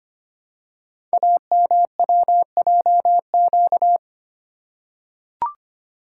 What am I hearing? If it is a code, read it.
Morse code, heard as a keyed tone, about 25 wpm, 700 Hz. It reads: AMWJQ
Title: Code Groups - 5 Letters - 25wpm